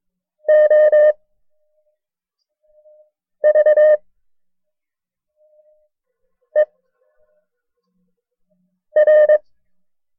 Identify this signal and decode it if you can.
Morse code, heard as a keyed tone, about 22 wpm, 610 Hz. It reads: O V E R
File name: FreeFiles / CW-practice / KB6MT-tape1-side2